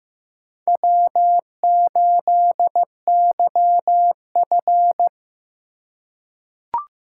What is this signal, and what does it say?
Morse code, heard as a keyed tone, about 15 wpm, 700 Hz. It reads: W8YF
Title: USA Call Signs - 4 Characters - 2x Character Spacing - 15wpm